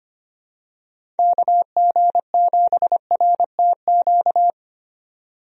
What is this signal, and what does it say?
Morse code, heard as a keyed tone, about 25 wpm, 700 Hz. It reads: KG7RTQ
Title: USA Call Signs - 6 Characters - 25wpm